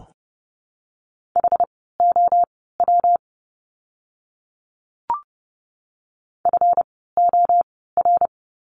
Morse code, heard as a keyed tone, about 30 wpm, spaced as for 17 wpm, 700 Hz. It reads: HOW FOR
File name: course / 30:10wpm / 237 Lesson 38f - Letter q - Words 30wpm